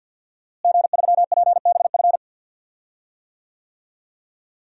Morse code, heard as a keyed tone, about 47 wpm, 700 Hz. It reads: G3PBF
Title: DXCC-100 Call Signs - 5 Characters - Repeated 3x - 50wpm